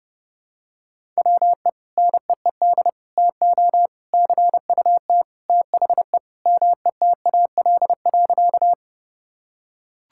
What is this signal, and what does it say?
Morse code, heard as a keyed tone, about 30 wpm, 700 Hz. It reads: WE NEED TO CUT THE METAL.
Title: Sentences from Top 1000 Words - Review Part 6 - 30wpm